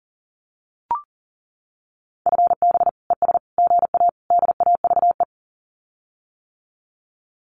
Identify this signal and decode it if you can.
Morse code, heard as a keyed tone, about 40 wpm, 700 Hz. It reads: FB ES GA DAVE
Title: Common QSO Phrases Encore 40wpm